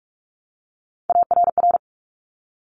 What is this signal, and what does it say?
Morse code, heard as a keyed tone, about 46 wpm, 700 Hz. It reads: ARR
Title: Top Three Letter Combinations in Words 45wpm